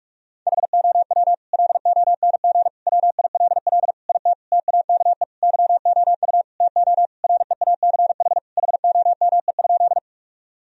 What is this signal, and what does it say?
Morse code, heard as a keyed tone, about 46 wpm, 700 Hz. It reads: HOW LONG WILL IT TAKE YOU TO REACH HOME?
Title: Sentences from Top 1000 Words - Review Part 2 - 45wpm